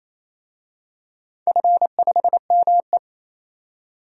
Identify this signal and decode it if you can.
Morse code, heard as a keyed tone, about 28 wpm, 700 Hz. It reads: F5ME
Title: DXCC-100 Call Signs - 4 Characters - 28wpm